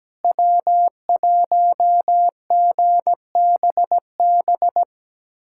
Morse code, heard as a keyed tone, about 17 wpm, 700 Hz. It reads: W1GBB
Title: USA Callsign DE Callsign 17wpm